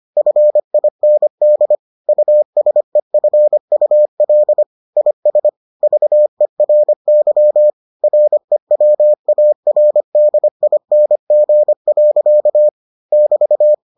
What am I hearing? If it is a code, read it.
Morse code, heard as a keyed tone, about 25 wpm, 600 Hz. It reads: FIND USEFUL IS VERY REWARDING. <BT>